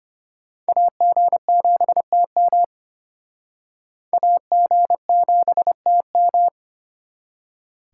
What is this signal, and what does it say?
Morse code, heard as a keyed tone, about 32 wpm, 700 Hz. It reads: AG7TM AG7TM